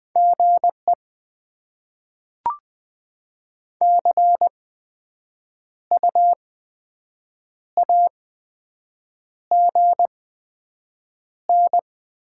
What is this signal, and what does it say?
Morse code, heard as a keyed tone, about 20 wpm, spaced as for 7 wpm, 700 Hz. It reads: GE CU AGN
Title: Top 20 QSO Elements - 8x Character Spacing - 20wpm